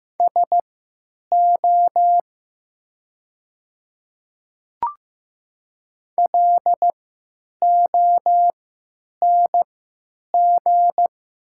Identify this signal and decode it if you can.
Morse code, heard as a keyed tone, about 15 wpm, spaced as for 8 wpm, 700 Hz. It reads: SO LONG
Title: 3x Character Spacing - Top 100 Words - 15wpm